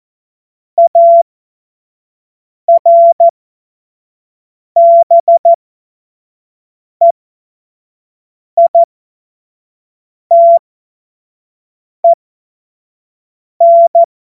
Morse code, heard as a keyed tone, about 14 wpm, spaced as for 5 wpm, 680 Hz. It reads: ARBEITEN
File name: Morsekurs / Witze / Witz 20. Juli 2021, 25 F70